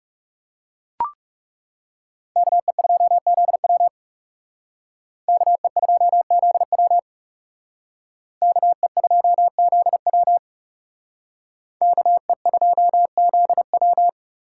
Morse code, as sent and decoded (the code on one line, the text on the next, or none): -.- . ..--- --.. .-- / -.- . ..--- --.. .-- / -.- . ..--- --.. .-- / -.- . ..--- --.. .--
KE2ZW KE2ZW KE2ZW KE2ZW